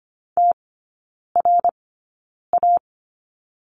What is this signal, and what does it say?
Morse code, heard as a keyed tone, about 25 wpm, spaced as for 9 wpm, 700 Hz. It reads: TRA
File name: Lesson 17e - Letter u - Three Characters 25wpm